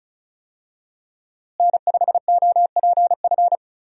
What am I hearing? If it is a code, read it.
Morse code, heard as a keyed tone, about 34 wpm, 700 Hz. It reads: N5OPF
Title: USA Call Signs - 5 Characters - Repeated 2x - 35wpm